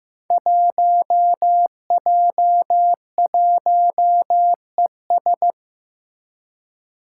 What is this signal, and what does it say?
Morse code, heard as a keyed tone, about 15 wpm, 700 Hz. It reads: 1J1ES
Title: Five Challenging Letter-Number Combinations 15wpm